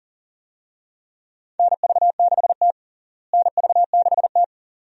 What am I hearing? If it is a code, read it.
Morse code, heard as a keyed tone, about 40 wpm, 700 Hz. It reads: NV6T NV6T
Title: USA Call Signs - 4 Characters - Repeated 2x - 40wpm